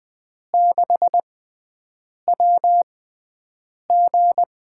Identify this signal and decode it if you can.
Morse code, heard as a keyed tone, about 20 wpm, spaced as for 7 wpm, 700 Hz. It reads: 6WG
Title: USA Call Signs - 4 Characters - 6x Character Spacing - 20wpm